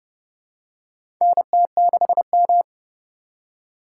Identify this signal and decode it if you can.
Morse code, heard as a keyed tone, about 30 wpm, 700 Hz. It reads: NT6M
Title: USA Call Signs - 4 Characters - 8x Character Spacing - 30wpm